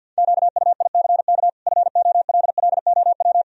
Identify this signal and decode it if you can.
Morse code, heard as a keyed tone, about 51 wpm, 700 Hz. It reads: QUICK FOLLOW
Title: Sentences from Top 1000 Words - Review Part 1 - 50wpm